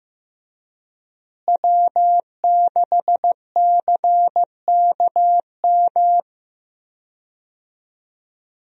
Morse code, heard as a keyed tone, about 15 wpm, 700 Hz. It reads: W6CKM
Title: USA Call Signs - 5 Characters - Repeated 3x - 15wpm